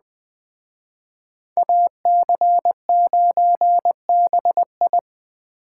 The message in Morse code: AC9BI

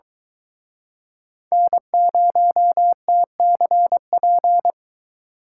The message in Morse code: N0TCP

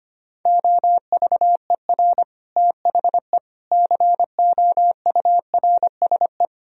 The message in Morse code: OVER THE COURSE

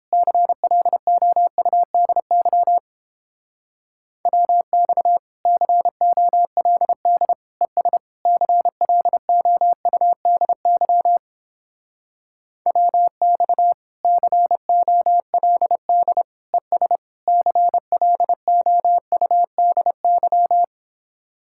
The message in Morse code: CLOUDY WX COLD ES CLOUDY WX COLD ES CLOUDY

